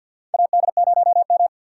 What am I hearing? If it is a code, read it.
Morse code, heard as a keyed tone, about 49 wpm, 700 Hz. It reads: AD0M
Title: USA Call Signs - 4 Characters - 6x Character Spacing - 50wpm